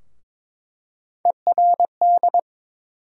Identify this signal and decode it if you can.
Morse code, heard as a keyed tone, about 22 wpm, 700 Hz. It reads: ERD
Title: USA Call Sign Postfixes - 3 Letters 22wpm